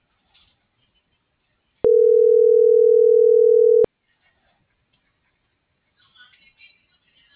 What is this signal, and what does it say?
Ringback tone (North America), 440 + 480 Hz.